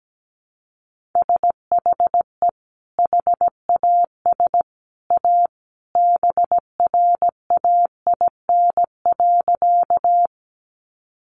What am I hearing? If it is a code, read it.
Morse code, heard as a keyed tone, about 17 wpm, 700 Hz. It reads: SHE HAS A BRAIN.
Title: Sentences from Top 1100 Words 17wpm